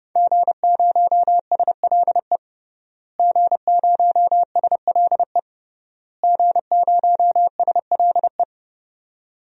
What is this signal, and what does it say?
Morse code, heard as a keyed tone, about 30 wpm, 700 Hz. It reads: G0SLE G0SLE G0SLE
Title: DXCC-100 Call Signs - 5 Characters - Repeated 3x - 30wpm